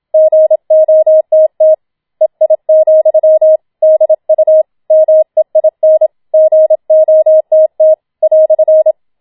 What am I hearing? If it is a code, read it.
Morse code, heard as a keyed tone, about 27 wpm, 630 Hz. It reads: GOTT EI, DU MEIN GOTT "